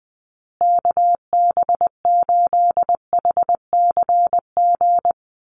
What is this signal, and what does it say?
Morse code, heard as a keyed tone, about 20 wpm, 700 Hz. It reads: KB8HCG
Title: USA Call Signs Repeated 2x 20wpm